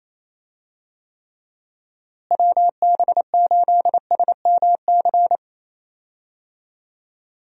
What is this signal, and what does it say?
Morse code, heard as a keyed tone, about 28 wpm, 700 Hz. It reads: WB8SMC
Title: USA Call Signs - 6 Characters - Repeated 2x - 28wpm